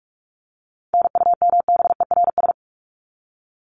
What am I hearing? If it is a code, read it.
Morse code, heard as a keyed tone, about 44 wpm, 700 Hz. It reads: NUMBERS
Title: ES:S Suffix - Top 100 Words with Base Words 45wpm